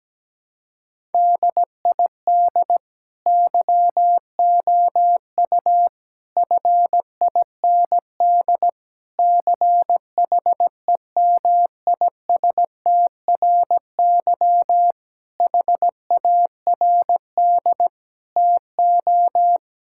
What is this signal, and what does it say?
Morse code, heard as a keyed tone, about 17 wpm, 700 Hz. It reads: DID YOU FIND CHEMISTRY HARD TO
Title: Sentences from Top 1000 Words - Review Part 6 - 17wpm